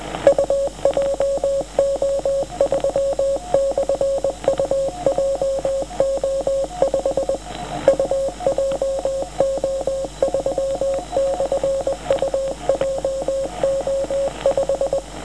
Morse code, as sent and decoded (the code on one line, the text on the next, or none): ..- .--- --- ...-- -..-. ..- .--- --- ..... / ..- .--- --- ...-- -..-. ..- .--- --- .....
UJO3/UJO5 UJO3/UJO5